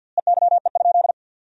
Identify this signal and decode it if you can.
Morse code, heard as a keyed tone, about 48 wpm, 700 Hz. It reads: EYE?
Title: 4x Word-Spacing - Sentences from Top 300 Words - 50wpm